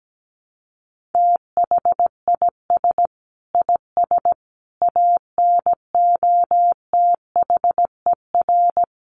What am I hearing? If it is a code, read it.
Morse code, heard as a keyed tone, about 17 wpm, 700 Hz. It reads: THIS IS ANOTHER
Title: Sentences from Top 600 Words 17wpm